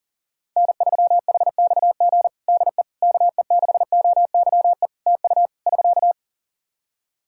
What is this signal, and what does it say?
Morse code, heard as a keyed tone, about 40 wpm, 700 Hz. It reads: N3HXG DE KE6OYE TU <SK>
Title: Practice QSO - Exchange 4 - Ending - 40wpm